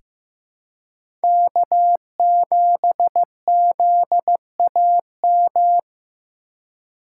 Morse code, heard as a keyed tone, about 15 wpm, 700 Hz. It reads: K7ZAM
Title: USA Call Signs - 5 Characters - 4x Character Spacing - 15wpm